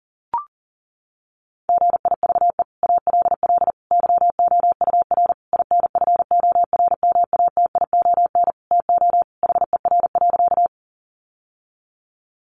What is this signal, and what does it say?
Morse code, heard as a keyed tone, about 40 wpm, 700 Hz. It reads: GIVE ALL YOUR INFORMATION TO HER.
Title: Sentences from Top 600 Words 40wpm